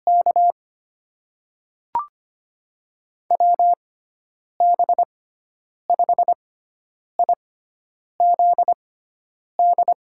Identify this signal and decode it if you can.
Morse code, heard as a keyed tone, about 25 wpm, spaced as for 9 wpm, 700 Hz. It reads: K WB5IZD
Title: USA Call Signs - 6x Character Spacing - 25wpm